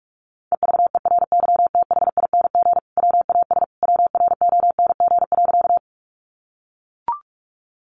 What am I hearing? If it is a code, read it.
Morse code, heard as a keyed tone, about 44 wpm, 700 Hz. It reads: EVERYTHING WAS WRONG.
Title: Sentences from Top 900 Words 45wpm